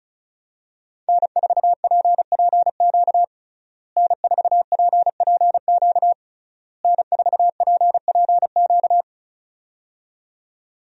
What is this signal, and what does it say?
Morse code, heard as a keyed tone, about 36 wpm, 700 Hz. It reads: N4PPQ N4PPQ N4PPQ
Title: USA Call Signs - 5 Characters - Repeated 3x - 35wpm